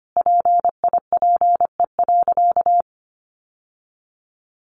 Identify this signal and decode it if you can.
Morse code, heard as a keyed tone, about 25 wpm, 700 Hz. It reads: PIPE.
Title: Sentences from Top 1500 Words 25wpm